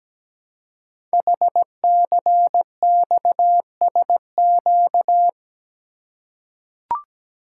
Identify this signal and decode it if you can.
Morse code, heard as a keyed tone, about 17 wpm, 700 Hz. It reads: HCXSQ
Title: Code Groups - 5 Letters - 17wpm